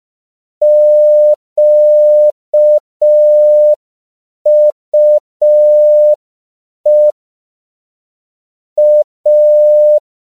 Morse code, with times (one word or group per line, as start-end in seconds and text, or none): 0.6-7.1 QUE
8.8-10.0 A